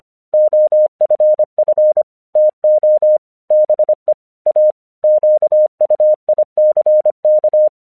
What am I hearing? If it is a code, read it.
Morse code, heard as a keyed tone, about 25 wpm, 610 Hz. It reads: OFF TO BE A QUICK